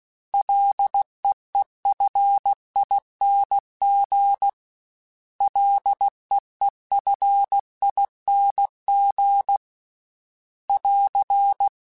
Morse code, heard as a keyed tone, about 16 wpm, 800 Hz. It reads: LEEFING LEEFING <AR>
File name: Prefixes 3x2 Twice @15